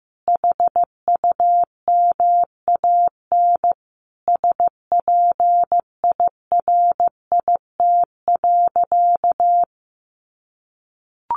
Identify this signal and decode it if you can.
Morse code, heard as a keyed tone, about 15 wpm, 700 Hz. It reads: HUMAN SPIRIT.
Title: Sentences from Top 1400 Words 15wpm